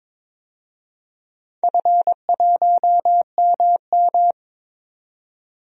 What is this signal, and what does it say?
Morse code, heard as a keyed tone, about 22 wpm, 700 Hz. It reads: F1MM
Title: DXCC-100 Call Signs - 4 Characters - 22wpm